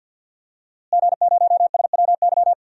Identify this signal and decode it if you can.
Morse code, heard as a keyed tone, about 52 wpm, 700 Hz. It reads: G0SWY
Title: DXCC-100 Call Signs - 5 Characters - Repeated 2x - 50wpm